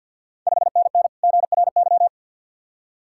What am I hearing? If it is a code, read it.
Morse code, heard as a keyed tone, about 49 wpm, 700 Hz. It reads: 5NN GRY
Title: Ontario QSO Party Exchange 50wpm